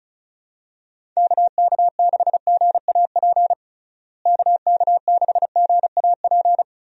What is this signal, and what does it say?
Morse code, heard as a keyed tone, about 36 wpm, 700 Hz. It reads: KK6GAP KK6GAP
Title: USA Call Signs - 6 Characters - Repeated 3x - 35wpm